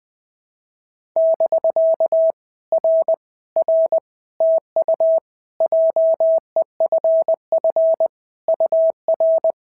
Morse code, heard as a keyed tone, about 19 wpm, 650 Hz. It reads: <BK> R R TU JEFF UR
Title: POTA QSO Practice - 20wpm